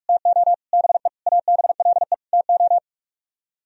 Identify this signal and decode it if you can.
Morse code, heard as a keyed tone, about 49 wpm, 700 Hz. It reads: TO BE ABLE TO